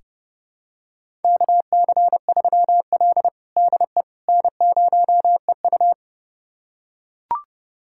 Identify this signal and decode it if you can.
Morse code, heard as a keyed tone, about 32 wpm, 700 Hz. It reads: KC3L DE N0EU
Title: USA Callsign DE Callsign - 4 Characters - 30wpm